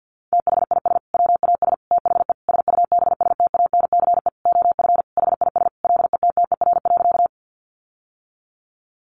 Morse code, heard as a keyed tone, about 50 wpm, 700 Hz. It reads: THIS WAS THE SUBSTANCE OF HIS LETTER.
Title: Sentences from Top 1500 Words 50wpm